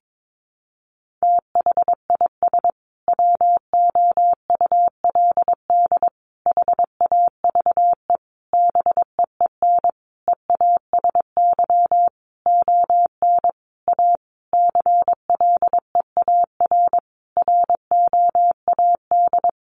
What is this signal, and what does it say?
Morse code, heard as a keyed tone, about 22 wpm, 700 Hz. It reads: THIS WOULD HAVE BEEN EASY ON A CLEAR ROAD
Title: Sentences from Top 800 Words 22wpm